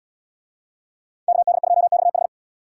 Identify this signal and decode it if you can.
Morse code, heard as a keyed tone, about 70 wpm, 700 Hz. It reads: KD2XH